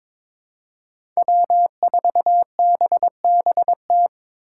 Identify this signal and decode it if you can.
Morse code, heard as a keyed tone, about 22 wpm, 700 Hz. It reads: W4BBT
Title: K1USN SST QSO Practice - 22wpm